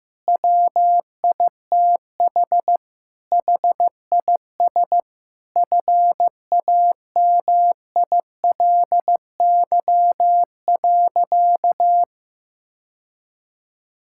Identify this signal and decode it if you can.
Morse code, heard as a keyed tone, about 15 wpm, 700 Hz. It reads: WITH HIS FAMILY.
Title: Sentences from Top 1600 Words 15wpm